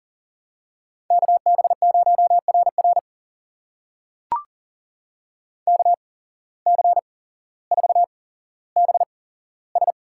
Morse code, heard as a keyed tone, about 40 wpm, spaced as for 17 wpm, 700 Hz. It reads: KB0RR KC4BS
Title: USA Call Signs - 8x Character Spacing - 40wpm